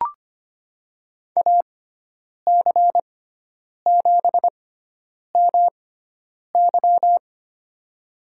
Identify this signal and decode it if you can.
Morse code, heard as a keyed tone, about 25 wpm, spaced as for 9 wpm, 700 Hz. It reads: AC7MY
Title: USA Call Signs - 6x Character Spacing - 25wpm